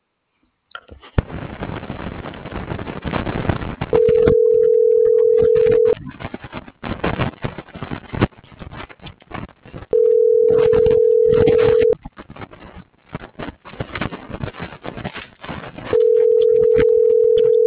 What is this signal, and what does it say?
Ringback tone (North America), 440 + 480 Hz.